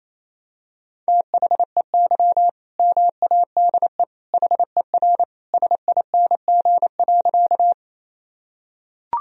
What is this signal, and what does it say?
Morse code, heard as a keyed tone, about 28 wpm, 700 Hz. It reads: THEY MADE HER SING.
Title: Sentences from Top 1000 Words - Review Part 2 - 28wpm